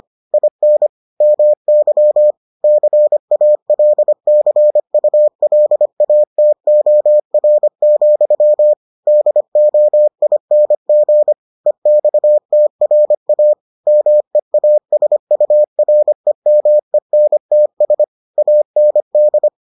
Morse code, heard as a keyed tone, about 25 wpm, 600 Hz. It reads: IN MY CALCULATOR, DOING EXTRA MEASUREMENTS AND